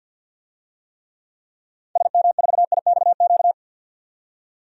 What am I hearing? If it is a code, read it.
Morse code, heard as a keyed tone, about 47 wpm, 700 Hz. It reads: SM4IXQ
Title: DXCC-100 Call Signs - 6 Characters - 50wpm